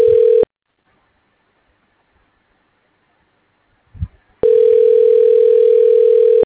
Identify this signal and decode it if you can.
Ringback tone (North America), 440 + 480 Hz.